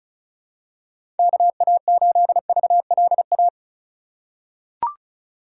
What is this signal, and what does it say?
Morse code, heard as a keyed tone, about 34 wpm, 700 Hz. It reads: KA8VLA
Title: USA Call Signs - 6 Characters - 3x Character Spacing - 35wpm